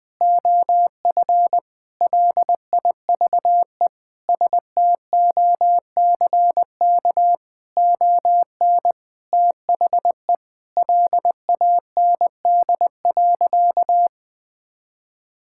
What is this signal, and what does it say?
Morse code, heard as a keyed tone, about 20 wpm, 700 Hz. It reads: OF LIVE STOCK ON THE LAND.
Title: Sentences from Top 1000 Words - Review Part 5 - 20wpm